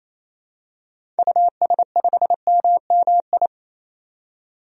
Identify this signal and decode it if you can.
Morse code, heard as a keyed tone, about 28 wpm, 700 Hz. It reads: US5MMI
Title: DXCC-100 Call Signs - 6 Characters - 28wpm